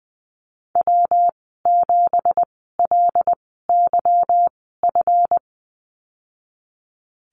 Morse code, heard as a keyed tone, about 19 wpm, 700 Hz. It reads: W 7 L Y F
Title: 207 Lesson 33g - Letter j - Call Signs 20wpm